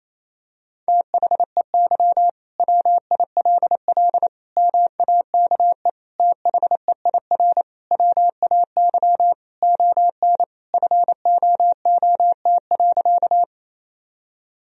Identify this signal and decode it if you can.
Morse code, heard as a keyed tone, about 27 wpm, 700 Hz. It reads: THEY WILL MAKE THEIR WAY ON FOOT.
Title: Sentences from Top 1000 Words - Review Part 3 - 28wpm